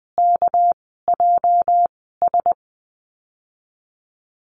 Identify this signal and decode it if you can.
Morse code, heard as a keyed tone, about 20 wpm, spaced as for 14 wpm, 700 Hz. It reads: KJS
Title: 213 Lesson 34g - Number 8 - Call Signs 20wpm